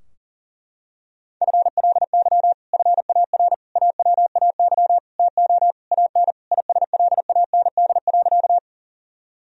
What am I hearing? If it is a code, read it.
Morse code, heard as a keyed tone, about 40 wpm, 700 Hz. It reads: FLY FAR AWAY TO AN ISLAND.
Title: Sentences from Top 1000 Words - Review Part 2 - 40wpm